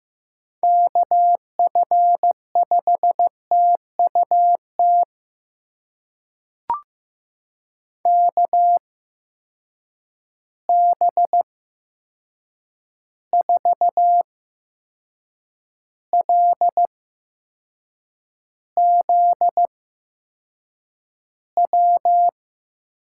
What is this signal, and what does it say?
Morse code, heard as a keyed tone, about 15 wpm, spaced as for 6 wpm, 700 Hz. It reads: KF5TUT KB4LZW
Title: USA Call Signs - 6 Characters - 8x Character Spacing - 15wpm